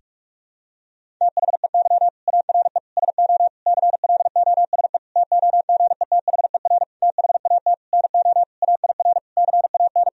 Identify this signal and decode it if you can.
Morse code, heard as a keyed tone, about 46 wpm, 700 Hz. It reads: THEY ARE SO CLOSE TOGETHER THAT NO AIR CAN
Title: Sentences from Top 1000 Words - Review Part 1 - 45wpm